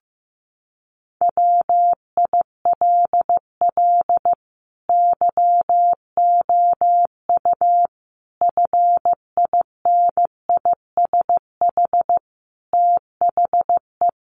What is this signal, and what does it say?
Morse code, heard as a keyed tone, about 15 wpm, 700 Hz. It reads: WILL YOU FINISH THE
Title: Sentences from Top 1300 Words 15wpm